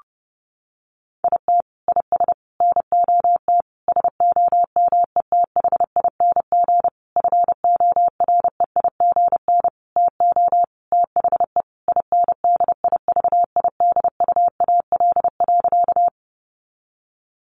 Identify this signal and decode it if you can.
Morse code, heard as a keyed tone, about 30 wpm, 700 Hz. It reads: IT IS NOT SOMETHING FOREIGN TO THE INDIVIDUAL.
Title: Sentences from Top 1400 Words 30wpm